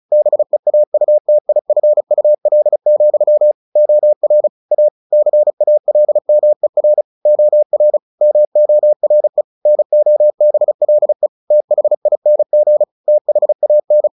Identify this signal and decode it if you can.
Morse code, heard as a keyed tone, about 34 wpm, 600 Hz. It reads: BEAUTIFUL, OR A CALMER OR MORE NOBLE THING THAN
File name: the old man and the sea 2 35wpm